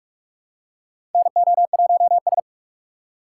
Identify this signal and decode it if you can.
Morse code, heard as a keyed tone, about 44 wpm, 700 Hz. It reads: NO1S
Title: USA Call Signs - 4 Characters - 4x Character Spacing - 45wpm